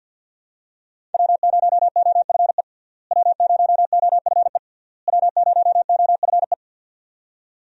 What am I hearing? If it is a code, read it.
Morse code, heard as a keyed tone, about 47 wpm, 700 Hz. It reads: W0OFE W0OFE W0OFE
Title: USA Call Signs - 5 Characters - Repeated 3x - 50wpm